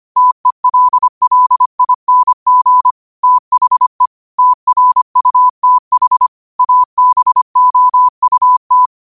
Morse code, heard as a keyed tone, about 25 wpm, 1000 Hz. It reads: TELLING THE TRUTH ABOUT
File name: qotdfiles / 25wpm